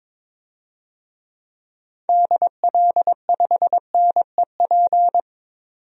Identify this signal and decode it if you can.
Morse code, heard as a keyed tone, about 22 wpm, 700 Hz. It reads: DL5NEP
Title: DXCC-100 Call Signs - 6 Characters - 22wpm